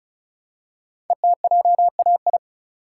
Morse code, heard as a keyed tone, about 34 wpm, 700 Hz. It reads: ETJAI
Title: Code Groups - 5 Letters - 35wpm